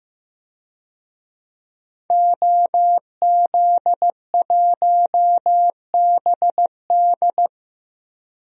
Morse code, heard as a keyed tone, about 15 wpm, 700 Hz. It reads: OZ1BD
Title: DXCC-100 Call Signs - 5 Characters - Repeated 2x - 15wpm